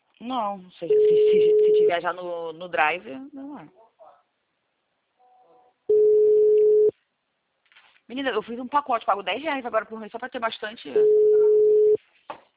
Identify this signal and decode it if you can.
Ringback tone (Europe, 425 Hz), with speech.
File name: out- -2002- - - .40274